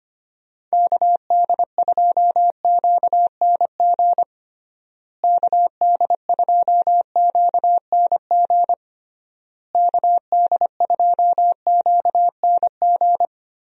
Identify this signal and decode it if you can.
Morse code, heard as a keyed tone, about 25 wpm, 700 Hz. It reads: KD2QNG KD2QNG KD2QNG
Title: USA Call Signs - 6 Characters - Repeated 3x - 25wpm